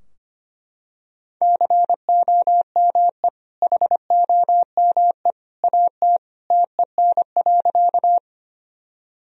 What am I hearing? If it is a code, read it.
Morse code, heard as a keyed tone, about 25 wpm, 700 Hz. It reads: COME HOME AT TEN.
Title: Sentences from Top 1000 Words - Review Part 2 - 25wpm